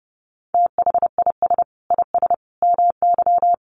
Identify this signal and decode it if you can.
Morse code, heard as a keyed tone, about 30 wpm, 700 Hz. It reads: THIS IS MY
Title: Sentences from Top 300 Words 30wpm